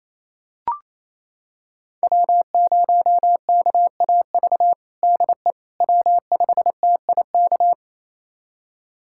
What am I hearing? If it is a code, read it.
Morse code, heard as a keyed tone, about 28 wpm, 700 Hz. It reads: W0KAV DE W5TIK
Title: USA Callsign DE Callsign - 5 Characters - 28wpm